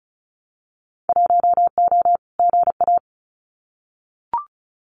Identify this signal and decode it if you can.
Morse code, heard as a keyed tone, about 34 wpm, 700 Hz. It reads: 1O GA
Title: Winter Field Day Exchanges 35wpm